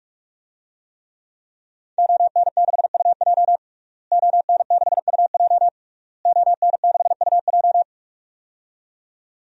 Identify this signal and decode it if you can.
Morse code, heard as a keyed tone, about 46 wpm, 700 Hz. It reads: ON6UJ ON6UJ ON6UJ
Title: DXCC-100 Call Signs - 5 Characters - Repeated 3x - 45wpm